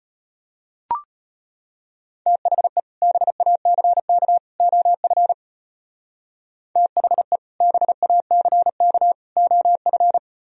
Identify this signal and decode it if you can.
Morse code, heard as a keyed tone, about 33 wpm, 700 Hz. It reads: THE BACK OF THE BACK OF